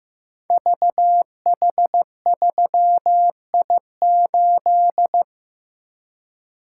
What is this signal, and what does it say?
Morse code, heard as a keyed tone, about 15 wpm, 700 Hz. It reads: VH3I8
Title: Code Groups - 5 Letter:Numbers - 15wpm